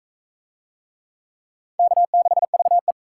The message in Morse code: K6VE

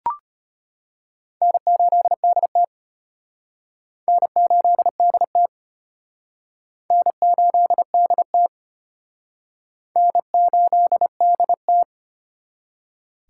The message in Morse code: N8DT N8DT N8DT N8DT

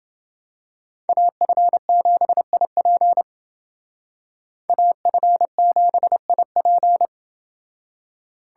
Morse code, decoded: AF7IP AF7IP

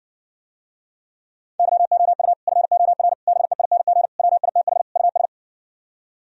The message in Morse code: YOU FOR BEING WITH US

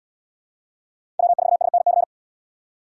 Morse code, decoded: W4SNC

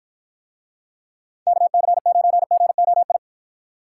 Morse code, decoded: KC9GOI

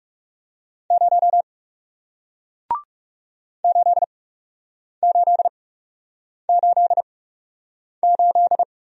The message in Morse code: 0 8888

